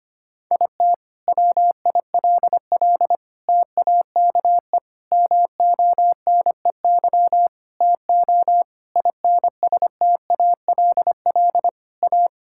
Morse code, heard as a keyed tone, about 25 wpm, 700 Hz. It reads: IT WILL TAKE MONEY TO INSTALL A
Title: Sentences from Top 1600 Words 25wpm